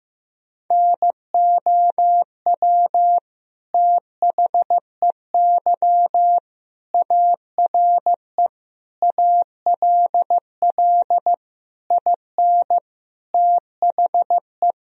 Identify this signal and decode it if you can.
Morse code, heard as a keyed tone, about 15 wpm, 700 Hz. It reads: NOW THEY ARE ALL IN THE
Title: Sentences from Top 1000 Words - Review Part 2 - 15wpm